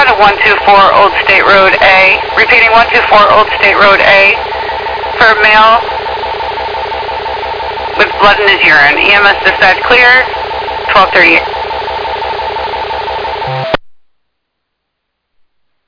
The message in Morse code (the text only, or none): none